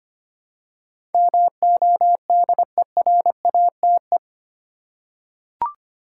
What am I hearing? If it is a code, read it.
Morse code, heard as a keyed tone, about 25 wpm, 700 Hz. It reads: MODERATE
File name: Eight Letter Words - 25wpm